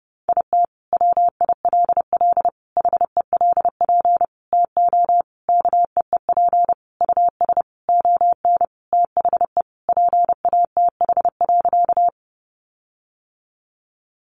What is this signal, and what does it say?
Morse code, heard as a keyed tone, about 30 wpm, 700 Hz. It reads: IT WILL HELP TO KEEP US ON THE PATH.
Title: Sentences from Top 1400 Words 30wpm